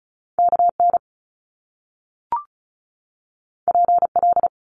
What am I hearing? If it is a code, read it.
Morse code, heard as a keyed tone, about 36 wpm, 700 Hz. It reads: KN PL
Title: Top Two Letter Combinations in Words 35wpm